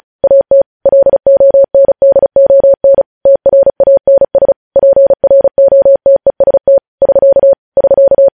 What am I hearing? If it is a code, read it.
Morse code, heard as a keyed tone, about 36 wpm, 550 Hz. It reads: AT LONDON TRANS PROTEST <SK> <SK>
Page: News Headlines in Morse Code at 35 WPM